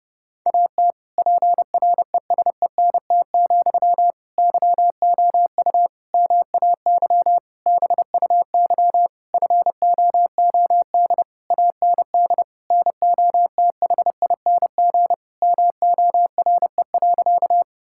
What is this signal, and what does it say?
Morse code, heard as a keyed tone, about 29 wpm, 700 Hz. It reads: AT PRESENT, YOU MAY BUY FOOD AND NOTHING MORE.
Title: Sentences from Top 1000 Words - Review Part 5 - 30wpm